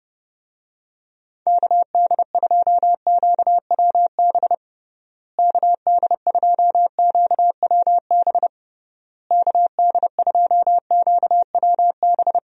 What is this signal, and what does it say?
Morse code, heard as a keyed tone, about 30 wpm, 700 Hz. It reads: KD2QWB KD2QWB KD2QWB
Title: USA Call Signs - 6 Characters - Repeated 3x - 30wpm